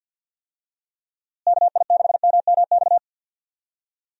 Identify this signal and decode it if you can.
Morse code, heard as a keyed tone, about 46 wpm, 700 Hz. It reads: KI6MMX
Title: USA Call Signs - 6 Characters - Repeated 3x - 50wpm